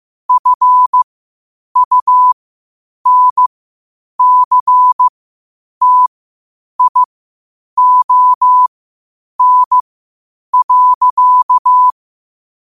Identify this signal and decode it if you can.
Morse code, heard as a keyed tone, about 15 wpm, spaced as for 8 wpm, 1000 Hz. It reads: FUNCTION.
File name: qotdfiles / 8wpm